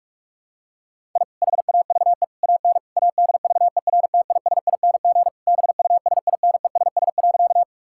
Morse code, heard as a keyed tone, about 46 wpm, 700 Hz. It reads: I HAVE AN ADVERTISING BUSINESS.
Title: Sentences from Top 1000 Words - Review Part 6 - 45wpm